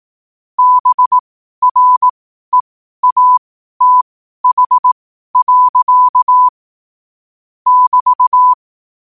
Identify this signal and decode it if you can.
Morse code, heard as a keyed tone, about 18 wpm, spaced as for 12 wpm, 1000 Hz. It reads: BREATH. <BT>
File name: qotdfiles / 12wpm